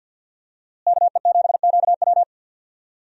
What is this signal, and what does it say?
Morse code, heard as a keyed tone, about 47 wpm, 700 Hz. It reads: KE7QW
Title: USA Call Signs - 5 Characters - Repeated 2x - 50wpm